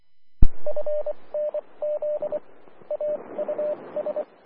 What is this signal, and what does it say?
Morse code, heard as a keyed tone, about 24 wpm, spaced as for 20 wpm, 600 Hz, with radio noise. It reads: FNZ AUS
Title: Austin, TX Amateur Repeaters